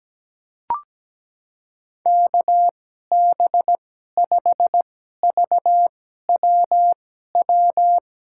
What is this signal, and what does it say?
Morse code, heard as a keyed tone, about 17 wpm, spaced as for 12 wpm, 700 Hz. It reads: KB5VWW
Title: USA Call Signs - 6 Characters - 2x Character Spacing - 17wpm